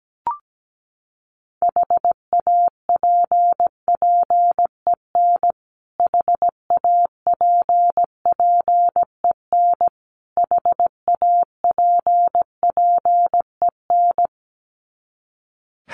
Morse code, heard as a keyed tone, about 17 wpm, 700 Hz. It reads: HAPPEN HAPPEN HAPPEN
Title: Top 500 Words Repeated 17wpm